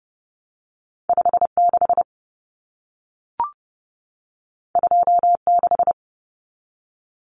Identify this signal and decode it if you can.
Morse code, heard as a keyed tone, about 30 wpm, 700 Hz. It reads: 56 26
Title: Numbers - 2-Digit 30wpm